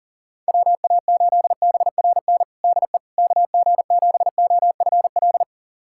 Morse code, heard as a keyed tone, about 37 wpm, 700 Hz. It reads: WA8BRN DE KG7OFL
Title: USA Callsign DE Callsign - 6 Characters - 40wpm